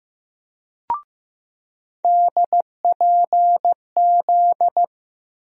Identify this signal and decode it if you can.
Morse code, heard as a keyed tone, about 15 wpm, 700 Hz. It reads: DPZ
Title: USA Call Sign Postfixes - 3 Letters 15wpm